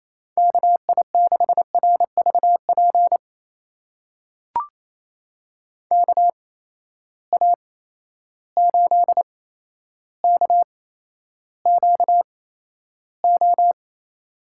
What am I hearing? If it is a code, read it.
Morse code, heard as a keyed tone, about 28 wpm, spaced as for 12 wpm, 700 Hz. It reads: KI6RVP KA8KQO
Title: USA Call Signs - 6 Characters - 8x Character Spacing - 28wpm